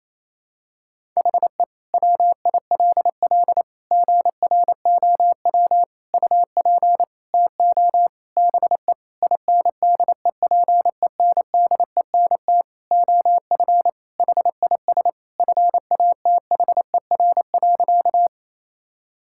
Morse code, heard as a keyed tone, about 28 wpm, 700 Hz. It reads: HE WILL GROW UP TO BE INDEPENDENT OF HIS FATHER.
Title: Sentences from Top 1000 Words - Review Part 6 - 28wpm